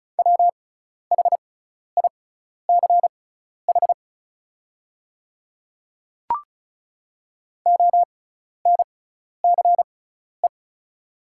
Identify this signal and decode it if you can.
Morse code, heard as a keyed tone, about 36 wpm, spaced as for 12 wpm, 700 Hz. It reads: WHICH ONCE
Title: 6x Character Spacing - Top 300 Words - 35wpm